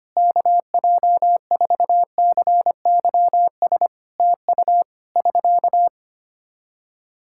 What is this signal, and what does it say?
Morse code, heard as a keyed tone, about 25 wpm, 700 Hz. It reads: KJ4CYS TU <SK>
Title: Practice QSO - Exchange 4 - Ending - 25wpm